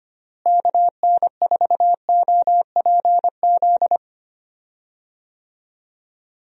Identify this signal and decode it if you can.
Morse code, heard as a keyed tone, about 25 wpm, 700 Hz. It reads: KN4OPZ
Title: USA Call Signs - 6 Characters - Repeated 2x - 25wpm